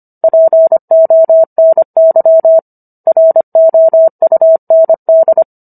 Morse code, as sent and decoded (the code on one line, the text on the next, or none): .--. --- -. -.-- / .-. --- ..- -. -..
PONY ROUND